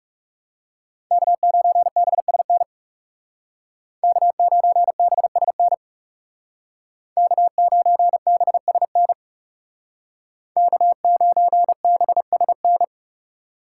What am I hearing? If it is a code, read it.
Morse code, heard as a keyed tone, about 39 wpm, 700 Hz. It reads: K9BSN K9BSN K9BSN K9BSN